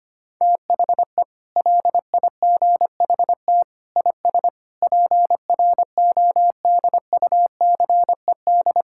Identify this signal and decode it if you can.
Morse code, heard as a keyed tone, about 25 wpm, 700 Hz. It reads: THE LIGHT IS PRODUCED
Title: Sentences from Top 1000 Words - Review Part 6 - 25wpm